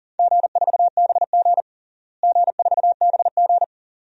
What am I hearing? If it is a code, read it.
Morse code, heard as a keyed tone, about 40 wpm, 700 Hz. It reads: G4BG G4BG
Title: DXCC-100 Call Signs - 4 Characters - Repeated 2x - 40wpm